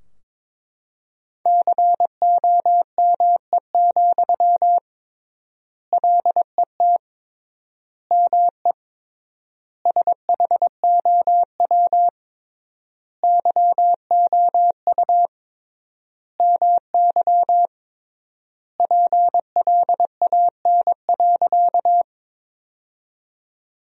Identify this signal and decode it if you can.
Morse code, heard as a keyed tone, about 22 wpm, 700 Hz. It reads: COME, LET ME SHOW YOU MY PLAN.
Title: 3x Word-Spacing - Sentences from Top 500 Words - 22wpm